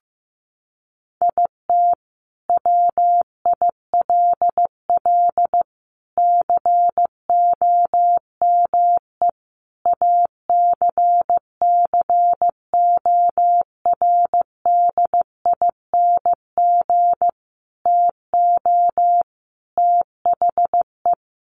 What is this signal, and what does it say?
Morse code, heard as a keyed tone, about 15 wpm, 700 Hz. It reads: IT WILL COME ACCORDING TO THE
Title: Sentences from Top 1400 Words 15wpm